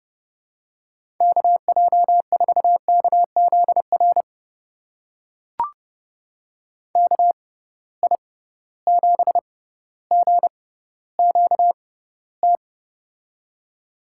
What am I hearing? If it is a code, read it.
Morse code, heard as a keyed tone, about 30 wpm, spaced as for 15 wpm, 700 Hz. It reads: KJ4KZR KI7GQT
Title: USA Call Signs - 6 Characters - 6x Character Spacing - 30wpm